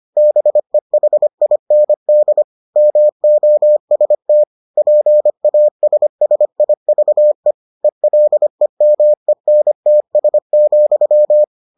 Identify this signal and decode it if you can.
Morse code, heard as a keyed tone, about 25 wpm, 600 Hz. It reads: BEHIND MOST PASSIVE ELEMENTS,